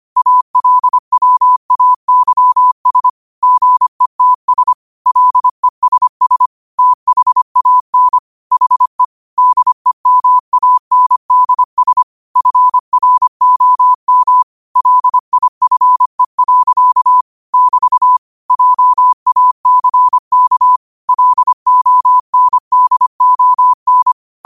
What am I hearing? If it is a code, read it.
Morse code, heard as a keyed tone, about 25 wpm, 1000 Hz. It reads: ALWAYS GETS LESS THAN HE DEMANDS FROM LIFE. <BT> JACK LONDON